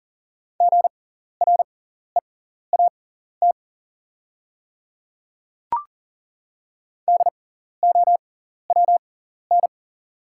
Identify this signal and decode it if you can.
Morse code, heard as a keyed tone, about 40 wpm, spaced as for 14 wpm, 690 Hz. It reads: GREAT DOWN
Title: 6x Character Spacing - Top 200 Words - 40wpm